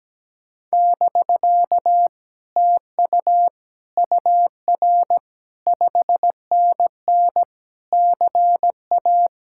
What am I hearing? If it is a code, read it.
Morse code, heard as a keyed tone, about 17 wpm, 700 Hz. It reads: <BK> TU UR 5NN CA
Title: Speed-Racing Common QSO Phrases - 17wpm